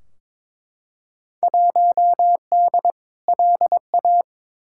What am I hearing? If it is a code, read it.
Morse code, heard as a keyed tone, about 22 wpm, 700 Hz. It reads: 1D LA
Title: Field Day Exchanges - 22wpm